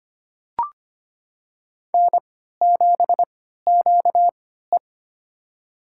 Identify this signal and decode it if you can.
Morse code, heard as a keyed tone, about 25 wpm, spaced as for 14 wpm, 700 Hz. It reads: N7QE
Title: USA Call Signs - 3x Character Spacing - 25wpm